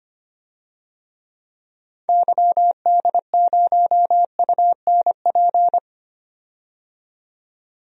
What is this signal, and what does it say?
Morse code, heard as a keyed tone, about 25 wpm, 700 Hz. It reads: YD0UNP
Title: DXCC-100 Call Signs - 6 Characters - 25wpm